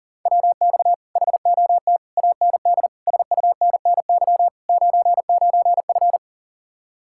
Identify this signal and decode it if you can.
Morse code, heard as a keyed tone, about 52 wpm, 700 Hz. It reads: WX HOT AND SUNNY 99F